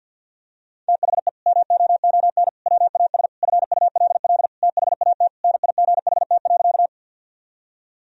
Morse code, heard as a keyed tone, about 52 wpm, 700 Hz. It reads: THE MOON WAS FULL THAT NIGHT.
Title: Sentences from Top 1000 Words - Review Part 3 - 50wpm